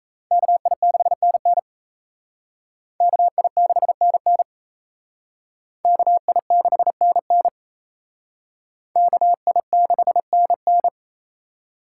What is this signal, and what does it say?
Morse code, heard as a keyed tone, about 34 wpm, 700 Hz. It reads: KI6NN KI6NN KI6NN KI6NN
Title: Speed Racing - USA 2x2 Call Signs 28wpm